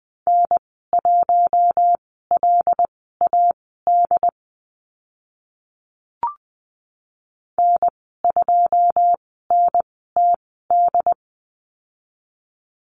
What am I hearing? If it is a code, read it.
Morse code, heard as a keyed tone, about 20 wpm, spaced as for 15 wpm, 700 Hz. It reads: N1LAD N2NTD
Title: 082 Lesson 14g - Number 2 - Call Signs 20wpm